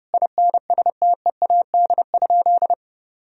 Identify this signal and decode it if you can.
Morse code, heard as a keyed tone, about 32 wpm, 700 Hz. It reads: INSTEAD?
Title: Sentences from Top 1000 Words - Review Part 4 - 30wpm